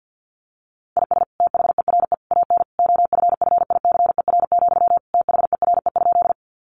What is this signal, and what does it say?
Morse code, heard as a keyed tone, about 50 wpm, 700 Hz. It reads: IS THERE AN OFFICER, THERE?
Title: Sentences from Top 1200 Words 50wpm